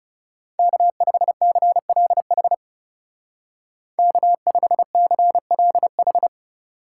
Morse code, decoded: K5CLH K5CLH

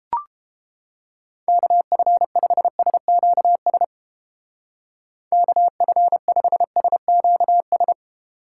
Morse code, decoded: KF5SQS KF5SQS

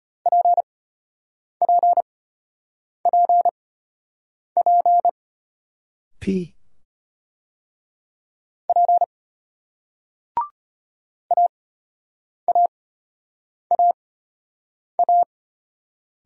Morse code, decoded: PPPP P AAAA

